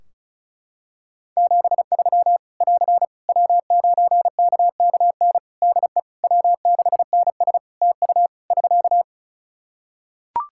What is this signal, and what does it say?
Morse code, heard as a keyed tone, about 34 wpm, 700 Hz. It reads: 73 <AR> W9KKN DE W6NS TU <SK>
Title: Practice QSO - Exchange 4 - Ending - 35wpm